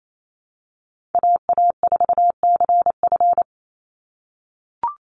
Morse code, heard as a keyed tone, about 28 wpm, 700 Hz. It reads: AA4CF
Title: Call Signs - 28wpm